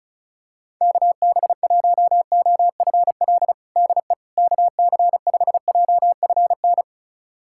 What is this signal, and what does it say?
Morse code, heard as a keyed tone, about 34 wpm, 700 Hz. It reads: KB1OFL DE KC5JFN
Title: USA Callsign DE Callsign - 6 Characters - 35wpm